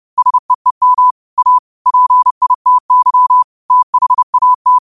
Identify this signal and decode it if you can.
Morse code, heard as a keyed tone, about 30 wpm, 1000 Hz. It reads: SEEM A PITY THAT
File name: qotdfiles / 30wpm